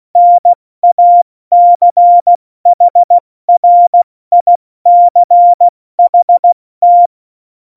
Morse code, heard as a keyed tone, about 16 wpm, 700 Hz. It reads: NACHRICHT
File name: Morsekurs / Witze / Witz 22. September 2020, 70 F80